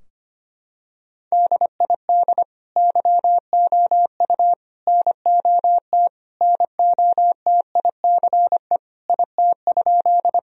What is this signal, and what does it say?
Morse code, heard as a keyed tone, about 25 wpm, 700 Hz. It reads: DID YOU NOT NOTICE IT?
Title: Sentences from Top 1000 Words - Review Part 2 - 25wpm